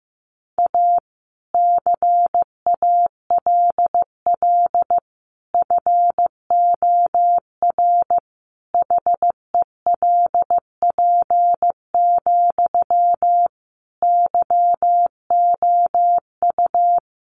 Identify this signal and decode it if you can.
Morse code, heard as a keyed tone, about 15 wpm, 700 Hz. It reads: A CALL FOR HELP, YOU
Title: Sentences from Top 300 Words 15wpm